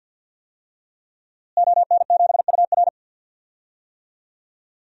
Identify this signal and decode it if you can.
Morse code, heard as a keyed tone, about 52 wpm, 700 Hz. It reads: ON7UR
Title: DXCC-100 Call Signs - 5 Characters - Repeated 3x - 50wpm